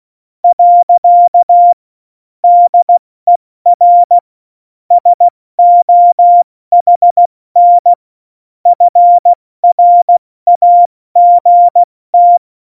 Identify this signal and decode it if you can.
Morse code, heard as a keyed tone, about 16 wpm, 700 Hz. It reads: . DER SOHN FRAGT
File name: Morsekurs / Witze / Witz 11. Juni 2019, 70 F80